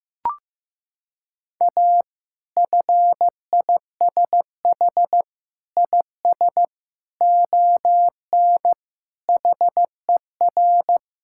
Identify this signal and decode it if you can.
Morse code, heard as a keyed tone, about 15 wpm, 700 Hz. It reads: A FISH IS ON HER
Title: Sentences from Top 1600 Words 15wpm